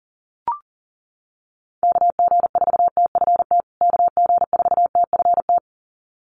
Morse code, heard as a keyed tone, about 40 wpm, 700 Hz. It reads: KG4TFT KG4TFT
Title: USA Call Signs Repeated 2x 40wpm